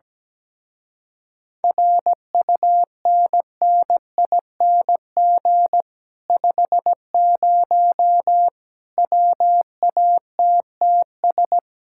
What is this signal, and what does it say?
Morse code, heard as a keyed tone, about 17 wpm, 700 Hz. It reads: RUNNING 50 WATTS
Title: Speed-Racing Common QSO Phrases Encore - 17wpm